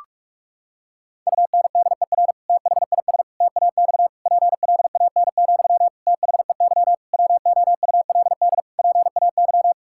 Morse code, heard as a keyed tone, about 47 wpm, 700 Hz. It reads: UNDER THIS TAX PLAN, THEY WOULD PAY
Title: Sentences from Top 1000 Words - Review Part 5 - 45wpm